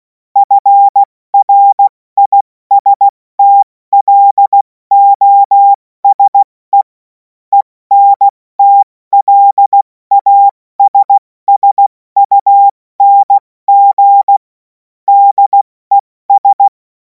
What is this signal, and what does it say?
Morse code, heard as a keyed tone, about 16 wpm, 800 Hz. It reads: FRISTLOSE ENTLASSUNG DES
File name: Morsekurs / Witze / Witz 03. Februar 2026 70 F80